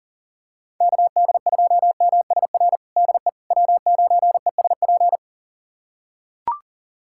Morse code, heard as a keyed tone, about 40 wpm, 700 Hz. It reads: KD2MSR DE W9ESP
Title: USA Callsign DE Callsign 40wpm